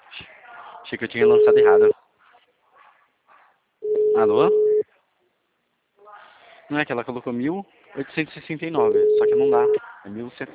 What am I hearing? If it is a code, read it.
Ringback tone (Europe, 425 Hz), with speech.